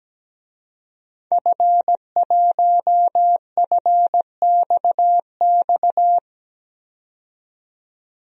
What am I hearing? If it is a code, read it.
Morse code, heard as a keyed tone, about 17 wpm, 700 Hz. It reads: F1FXX
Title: DXCC-100 Call Signs - 5 Characters - 17wpm